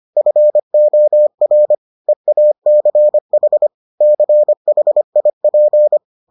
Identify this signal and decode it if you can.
Morse code, heard as a keyed tone, about 25 wpm, 600 Hz. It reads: FOR EACH CHIP